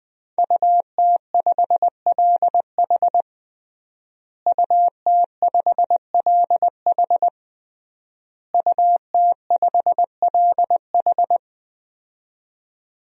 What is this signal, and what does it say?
Morse code, heard as a keyed tone, about 20 wpm, 700 Hz. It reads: UT5LH UT5LH UT5LH
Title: DXCC-100 Call Signs - 5 Characters - Repeated 3x - 20wpm